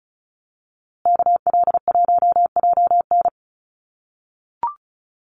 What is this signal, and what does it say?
Morse code, heard as a keyed tone, about 34 wpm, 700 Hz. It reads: KL1JN
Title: USA Call Signs Repeated 3x 35wpm